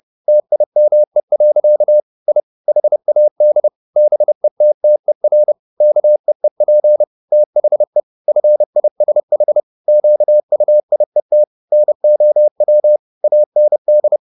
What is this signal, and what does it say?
Morse code, heard as a keyed tone, about 30 wpm, 600 Hz. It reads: TIME. I HAD BETTER KEEP THE FISH QUIET NOW AND